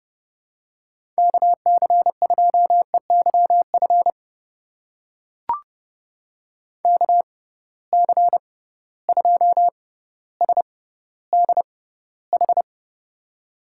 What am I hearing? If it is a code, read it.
Morse code, heard as a keyed tone, about 30 wpm, spaced as for 15 wpm, 700 Hz. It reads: KC2EYF KC2SDH
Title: USA Call Signs - 6 Characters - 6x Character Spacing - 30wpm